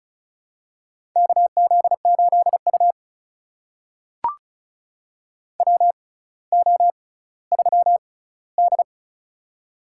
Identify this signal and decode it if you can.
Morse code, heard as a keyed tone, about 34 wpm, spaced as for 18 wpm, 700 Hz. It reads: KZ8U WO3D
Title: USA Call Signs - 4 Characters - 6x Character Spacing - 35wpm